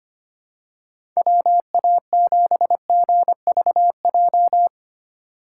Morse code, heard as a keyed tone, about 25 wpm, 700 Hz. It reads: WA7GVJ
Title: USA Call Signs - 2x Character Spacing - 25wpm